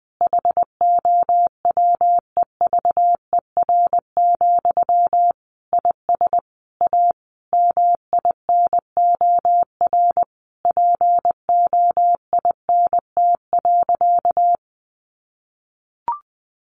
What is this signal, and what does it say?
Morse code, heard as a keyed tone, about 20 wpm, 700 Hz. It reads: HOWEVER, IS A MINOR POINT.
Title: Sentences from Top 1500 Words 20wpm